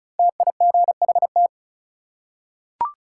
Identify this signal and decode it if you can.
Morse code, heard as a keyed tone, about 34 wpm, 700 Hz. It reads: TIGHT
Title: Five Letter Words - 35wpm